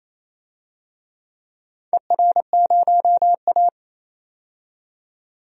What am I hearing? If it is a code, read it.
Morse code, heard as a keyed tone, about 28 wpm, 700 Hz. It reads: ER0A
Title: DXCC-100 Call Signs - 4 Characters - 28wpm